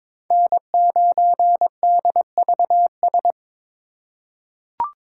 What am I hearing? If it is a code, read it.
Morse code, heard as a keyed tone, about 22 wpm, 700 Hz. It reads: N9DVS
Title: USA Call Signs - 5 Characters - 4x Character Spacing - 22wpm